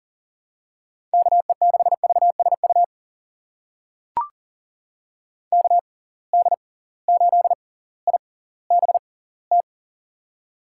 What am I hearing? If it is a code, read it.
Morse code, heard as a keyed tone, about 37 wpm, spaced as for 20 wpm, 700 Hz. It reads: KE6VSU KD8IBT
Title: USA Call Signs - 6 Characters - 6x Character Spacing - 40wpm